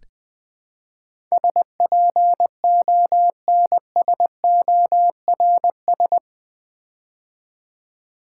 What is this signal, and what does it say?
Morse code, heard as a keyed tone, about 20 wpm, 700 Hz. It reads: SPONSORS
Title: Build A Word - 4 Letters At A Time - 11-Letter Words - 20wpm